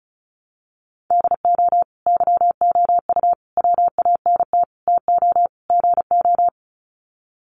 Morse code, heard as a keyed tone, about 34 wpm, 700 Hz. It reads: DO YOU WANT TO GO
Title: Sets of 5 Words Encore2 35wpm